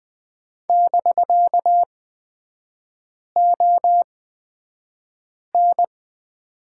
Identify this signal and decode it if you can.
Morse code, heard as a keyed tone, about 20 wpm, spaced as for 5 wpm, 700 Hz. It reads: <BK>ON
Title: 250 Lesson 40e - BK - Three Characters 20wpm